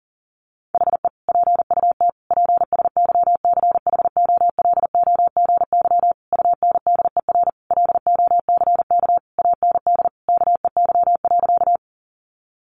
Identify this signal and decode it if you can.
Morse code, heard as a keyed tone, about 40 wpm, 700 Hz. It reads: HE PUT PSYCHOLOGY UNDER LOCK AND KEY.
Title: Sentences from Top 1400 Words 40wpm